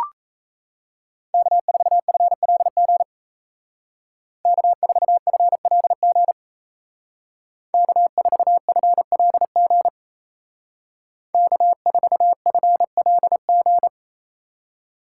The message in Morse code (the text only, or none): K4FLG K4FLG K4FLG K4FLG